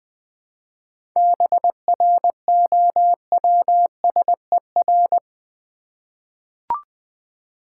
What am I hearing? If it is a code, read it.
Morse code, heard as a keyed tone, about 20 wpm, 700 Hz. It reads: BROWSER
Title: 7 Letter Words - First 3 Letters Given - 20wpm